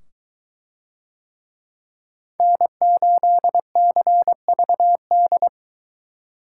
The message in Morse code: N8CVD

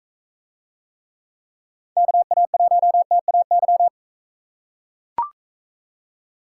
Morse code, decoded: KA1TAY

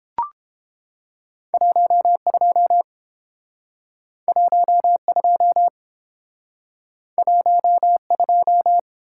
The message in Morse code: 12 12 12

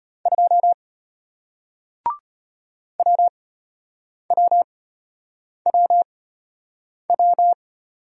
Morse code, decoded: 2 WWWW

